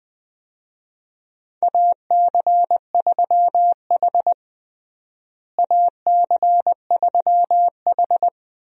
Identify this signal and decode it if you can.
Morse code, heard as a keyed tone, about 21 wpm, 700 Hz. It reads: AC3H AC3H
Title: USA Call Signs - 4 Characters - Repeated 2x - 20wpm